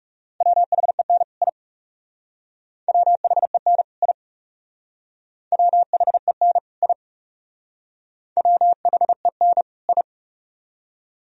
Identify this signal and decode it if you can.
Morse code, heard as a keyed tone, about 36 wpm, 700 Hz. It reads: WHEN I WHEN I WHEN I WHEN I
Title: Speed-Racing - Sets of 2 Words 30wpm